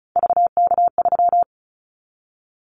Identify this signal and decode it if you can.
Morse code, heard as a keyed tone, about 36 wpm, 700 Hz. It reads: VK3
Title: Callsign Prefixes for Top 50 DXCC - 35wpm